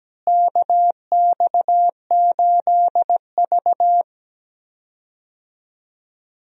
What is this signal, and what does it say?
Morse code, heard as a keyed tone, about 17 wpm, 700 Hz. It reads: KX8V
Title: USA Call Signs - 4 Characters - Repeated 2x - 17wpm